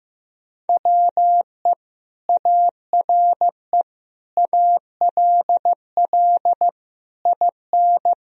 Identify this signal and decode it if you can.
Morse code, heard as a keyed tone, about 16 wpm, 700 Hz. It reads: WE ARE ALL IN
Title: Sentences from Top 1000 Words - Review Part 3 - 15wpm